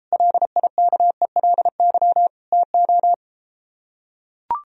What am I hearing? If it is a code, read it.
Morse code, heard as a keyed tone, about 33 wpm, 700 Hz. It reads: LIKELY TO